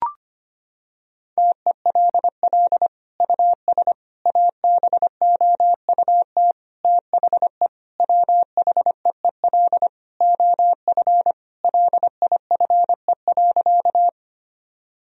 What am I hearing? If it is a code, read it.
Morse code, heard as a keyed tone, about 25 wpm, 700 Hz. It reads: TELL US ABOUT THE WHEEL OF LIFE.
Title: Sentences from Top 1000 Words - Review Part 3 - 25wpm